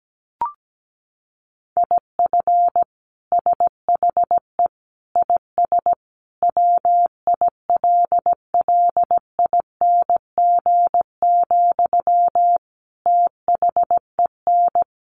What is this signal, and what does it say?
Morse code, heard as a keyed tone, about 17 wpm, 700 Hz. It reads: IF SHE IS WILLING, THEN
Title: Sentences from Top 1000 Words 17wpm